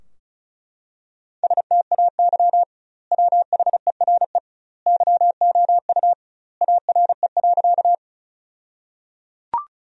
Morse code, heard as a keyed tone, about 36 wpm, 700 Hz. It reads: STAY WHERE YOU ARE.
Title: 2x Word-Spacing - Sentences from Top 500 Words - 35wpm